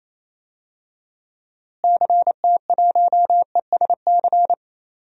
Morse code, heard as a keyed tone, about 28 wpm, 700 Hz. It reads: CT1ESC
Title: DXCC-100 Call Signs - 6 Characters - Repeated 2x - 28wpm